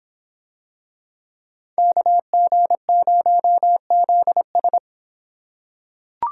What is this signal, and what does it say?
Morse code, heard as a keyed tone, about 25 wpm, 700 Hz. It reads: KG0ZS